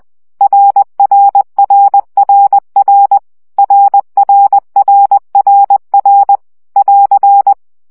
Morse code, heard as a keyed tone, about 20 wpm, 800 Hz. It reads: RRRRR RRRRR <AR>
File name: CW gyakorlas r 17 baud